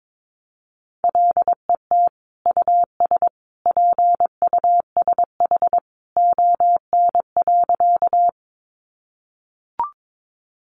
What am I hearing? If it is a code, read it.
Morse code, heard as a keyed tone, about 22 wpm, 700 Hz. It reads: LET US PUSH ON.
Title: Sentences from Top 1300 Words 22wpm